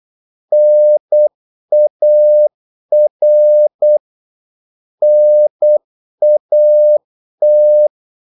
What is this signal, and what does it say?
Morse code, heard as a keyed tone, about 8 wpm, 600 Hz. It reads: NAR NAT
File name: Korte ord DK 8wpm